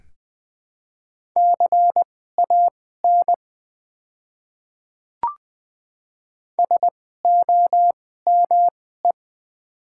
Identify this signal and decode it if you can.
Morse code, heard as a keyed tone, about 20 wpm, spaced as for 14 wpm, 700 Hz. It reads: CAN SOME
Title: 2x Character Spacing - Top 300 Words - 20wpm